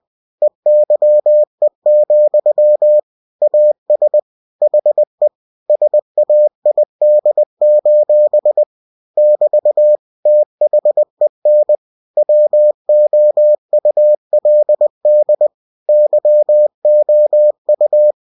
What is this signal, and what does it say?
Morse code, heard as a keyed tone, about 20 wpm, 600 Hz. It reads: EYE, AS HE SAID: <BT> THEN WOULD YOU
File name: cw-books / winnie-the-pooh / winnie-20wpm-page-0019